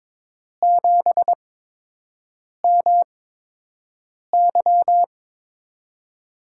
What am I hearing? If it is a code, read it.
Morse code, heard as a keyed tone, about 22 wpm, spaced as for 6 wpm, 700 Hz. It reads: 7MY
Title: USA Call Signs - 5 Characters - 8x Character Spacing - 22wpm